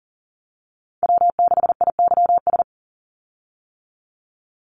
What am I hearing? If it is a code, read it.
Morse code, heard as a keyed tone, about 41 wpm, 700 Hz. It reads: W6IYS
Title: Call Signs - 40wpm